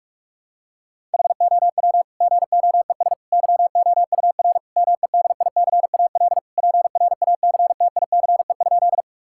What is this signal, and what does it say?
Morse code, heard as a keyed tone, about 41 wpm, 700 Hz. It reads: HOW GOES YOUR MEDICAL PRACTICE?
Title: Sentences from Top 1000 Words - Review Part 5 - 45wpm